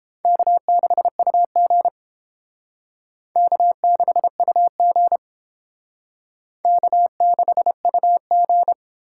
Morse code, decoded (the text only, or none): K6UG K6UG K6UG